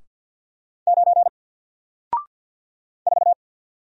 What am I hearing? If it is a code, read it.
Morse code, heard as a keyed tone, about 50 wpm, 700 Hz. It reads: 9 4
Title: Single Letter-Number Alt 50wpm